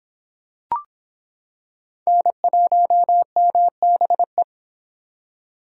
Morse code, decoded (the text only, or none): N1MBE